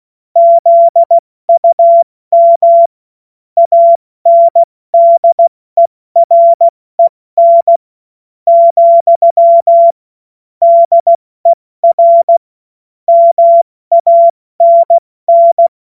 Morse code, heard as a keyed tone, about 16 wpm, spaced as for 13 wpm, 680 Hz. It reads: ZUM ANDEREN, DER MANN
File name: Morsekurs / Witze / Witz 16. August 70 F80